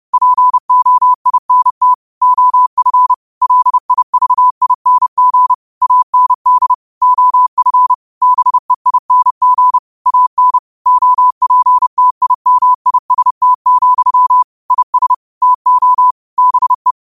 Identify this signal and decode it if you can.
Morse code, heard as a keyed tone, about 30 wpm, 1000 Hz. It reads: POINT OF LIVING AND OF BEING AN OPTIMIST, IS TO BE
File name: qotdfiles / 30wpm